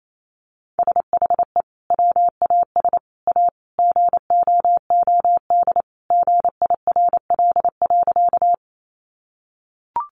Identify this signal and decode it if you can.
Morse code, heard as a keyed tone, about 28 wpm, 700 Hz. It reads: SHE WAS A GOOD GIRL.
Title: Sentences from Top 400 Words 28wpm